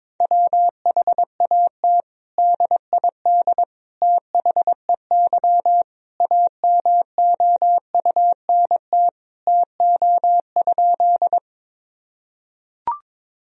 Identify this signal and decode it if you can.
Morse code, heard as a keyed tone, about 22 wpm, 700 Hz. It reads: WHAT DID THEY AMOUNT TO?
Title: Sentences from Top 1000 Words - Review Part 4 - 22wpm